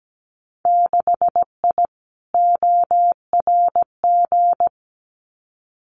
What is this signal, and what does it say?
Morse code, heard as a keyed tone, about 17 wpm, 700 Hz. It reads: 6I ORG
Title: Winter Field Day Exchanges Encore 17wpm